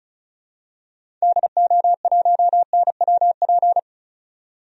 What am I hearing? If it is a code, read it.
Morse code, heard as a keyed tone, about 36 wpm, 700 Hz. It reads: DO1NWP
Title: DXCC-100 Call Signs - 6 Characters - 35wpm